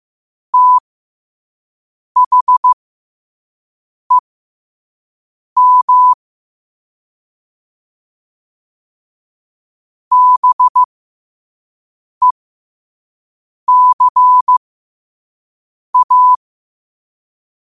Morse code, heard as a keyed tone, about 15 wpm, spaced as for 5 wpm, 1000 Hz. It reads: THEM BECA
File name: qotdfiles / 5wpm